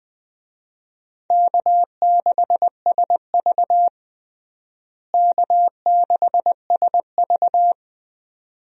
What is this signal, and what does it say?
Morse code, heard as a keyed tone, about 21 wpm, 700 Hz. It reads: K6SV K6SV
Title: USA Call Signs - 4 Characters - Repeated 2x - 20wpm